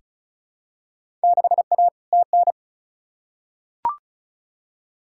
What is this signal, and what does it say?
Morse code, heard as a keyed tone, about 36 wpm, 700 Hz. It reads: 6A TN
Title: Field Day Exchanges Encore - 35wpm